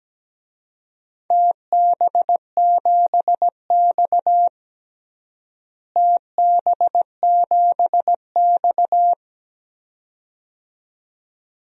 Morse code, heard as a keyed tone, about 17 wpm, 700 Hz. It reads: TB7X TB7X
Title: DXCC-100 Call Signs - 4 Characters - Repeated 2x - 17wpm